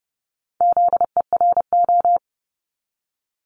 Morse code, heard as a keyed tone, about 30 wpm, 700 Hz. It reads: ZERO
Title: Multi-letter Phonemes 30wpm